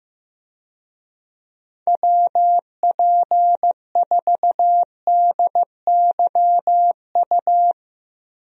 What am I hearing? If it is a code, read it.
Morse code, heard as a keyed tone, about 16 wpm, 700 Hz. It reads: WP4DYU
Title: USA Call Signs - 6 Characters - Repeated 2x - 15wpm